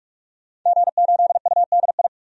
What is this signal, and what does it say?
Morse code, heard as a keyed tone, about 42 wpm, 700 Hz. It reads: G8UDI
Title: DXCC-100 Call Signs - 5 Characters - 45wpm